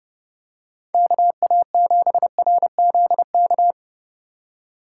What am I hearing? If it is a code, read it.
Morse code, heard as a keyed tone, about 27 wpm, 700 Hz. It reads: KA7RZK